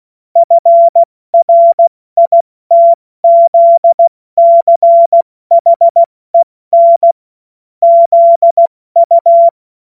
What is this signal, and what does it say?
Morse code, heard as a keyed tone, about 16 wpm, spaced as for 14 wpm, 680 Hz. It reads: FRITZCHEN ZU
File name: Morsekurs / Witze / Witz 13. September 2022 70 F80